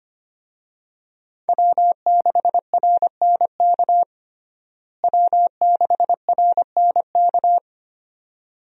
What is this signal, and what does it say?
Morse code, heard as a keyed tone, about 25 wpm, 700 Hz. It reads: W6RNK W6RNK
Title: USA Call Signs - 5 Characters - Repeated 2x - 25wpm